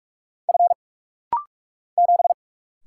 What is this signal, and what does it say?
Morse code, heard as a keyed tone, about 44 wpm, 690 Hz. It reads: F 7
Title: Single Letter-Number - Rapid-Fire 45wpm